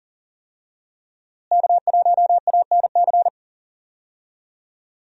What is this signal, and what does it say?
Morse code, heard as a keyed tone, about 37 wpm, 700 Hz. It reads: K1ANC
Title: USA Call Signs - 5 Characters - Repeated 2x - 40wpm